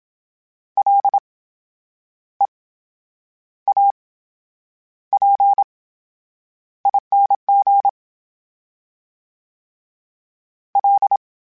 Morse code, heard as a keyed tone, about 27 wpm, spaced as for 9 wpm, 800 Hz. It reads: LEAPING L
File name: Special INGs / Special INGs-mp3s / VerySlowToFaster Ing @25